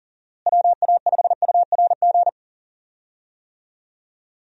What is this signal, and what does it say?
Morse code, heard as a keyed tone, about 40 wpm, 700 Hz. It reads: WA5URG
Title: USA Call Signs - 6 Characters - Repeated 2x - 40wpm